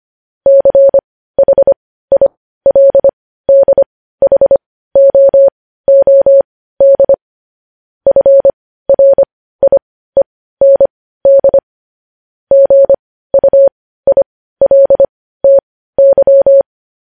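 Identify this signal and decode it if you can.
Morse code, heard as a keyed tone, about 25 wpm, spaced as for 15 wpm, 550 Hz. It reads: CHILDHOOD FRIEND GUILTY
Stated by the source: Morse news headlines sent at 15 wpm.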